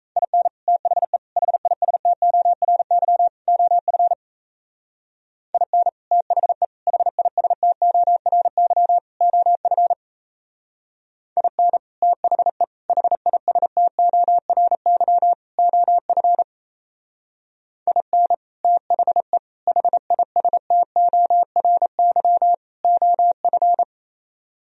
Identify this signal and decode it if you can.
Morse code, heard as a keyed tone, about 35 wpm, 700 Hz. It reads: IN THE HISTORY OF IN THE HISTORY OF IN THE HISTORY OF IN THE HISTORY OF